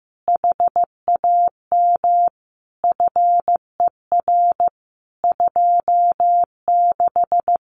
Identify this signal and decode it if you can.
Morse code, heard as a keyed tone, about 15 wpm, 700 Hz. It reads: HAM FER 26
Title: Common QSO Phrases Encore 15wpm